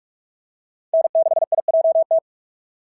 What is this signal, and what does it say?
Morse code, heard as a keyed tone, about 46 wpm, 650 Hz. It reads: N6IJT
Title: SOTA QSO Practice 45wpm